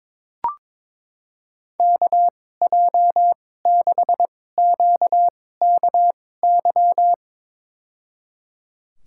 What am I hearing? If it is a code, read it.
Morse code, heard as a keyed tone, about 22 wpm, spaced as for 16 wpm, 700 Hz. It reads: KJ6QKY
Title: USA Call Signs - 6 Characters - 2x Character Spacing - 22wpm